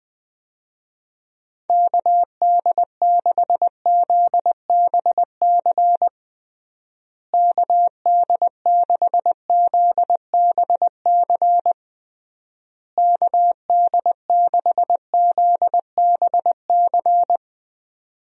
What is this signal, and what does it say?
Morse code, heard as a keyed tone, about 20 wpm, 700 Hz. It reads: KD6ZBC KD6ZBC KD6ZBC
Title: USA Call Signs - 6 Characters - Repeated 3x - 20wpm